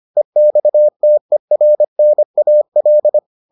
Morse code, heard as a keyed tone, about 25 wpm, 600 Hz. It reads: EXTERNAL